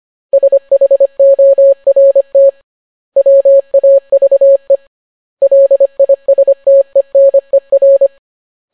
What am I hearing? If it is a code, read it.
Morse code, heard as a keyed tone, about 25 wpm, 550 Hz. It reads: SHORT WAVE LISTENER